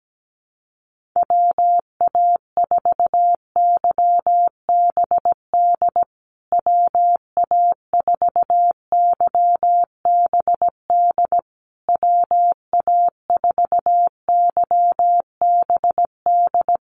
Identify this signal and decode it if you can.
Morse code, heard as a keyed tone, about 17 wpm, 700 Hz. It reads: WA4YBD WA4YBD WA4YBD
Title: USA Call Signs Repeated 3x 17wpm